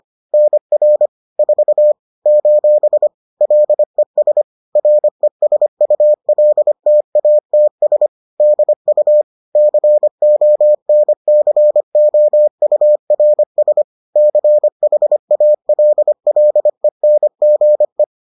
Morse code, heard as a keyed tone, about 25 wpm, 600 Hz. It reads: NR 4 : LES RESULTATS DU CONCOURS CHALLENGE
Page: bulletin UFT sous forme texte et en CW